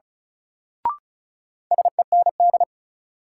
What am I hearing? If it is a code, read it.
Morse code, heard as a keyed tone, about 36 wpm, 700 Hz. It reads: SEND
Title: Sentences from Top 1500 Words - New Word Drill Alt - 35wpm